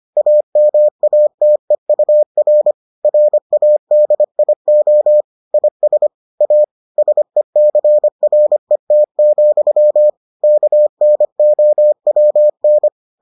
Morse code, heard as a keyed tone, about 25 wpm, 600 Hz. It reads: AMATEUR RADIO IS A SECRET, KNOWN